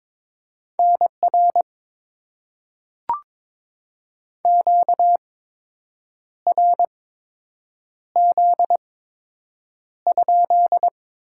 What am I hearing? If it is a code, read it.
Morse code, heard as a keyed tone, about 22 wpm, spaced as for 7 wpm, 700 Hz. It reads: NR QRZ?
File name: Top 90 QSO Elements - 8x Character Spacing - 22wpm